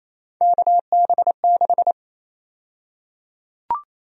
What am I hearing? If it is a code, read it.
Morse code, heard as a keyed tone, about 28 wpm, 700 Hz. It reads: KB6
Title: USA Call Sign Prefixes - Repeated 2x - 28wpm